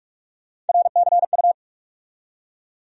Morse code, heard as a keyed tone, about 42 wpm, 700 Hz. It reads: RCU
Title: USA Call Sign Postfixes - 3 Letters 45wpm